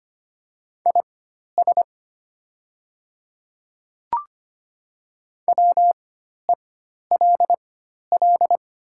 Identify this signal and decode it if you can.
Morse code, heard as a keyed tone, about 25 wpm, spaced as for 12 wpm, 700 Hz. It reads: IS WELL
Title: 4x Character Spacing - Top 200 Words - 25wpm